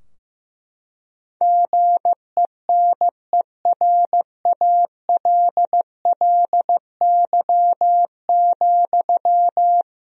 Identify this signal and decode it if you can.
Morse code, heard as a keyed tone, about 15 wpm, 700 Hz. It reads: GENERALLY,
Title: Sentences from Top 1000 Words - Review Part 4 - 15wpm